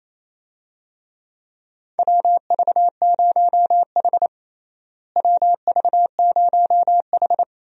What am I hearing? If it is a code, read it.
Morse code, heard as a keyed tone, about 27 wpm, 700 Hz. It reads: WV0H WV0H
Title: USA Call Signs - 4 Characters - Repeated 3x - 28wpm